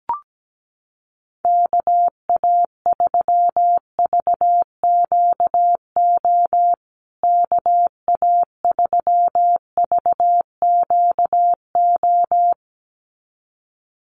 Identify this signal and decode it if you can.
Morse code, heard as a keyed tone, about 17 wpm, 700 Hz. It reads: KA3VQO KA3VQO
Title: USA Call Signs Repeated 2x 17wpm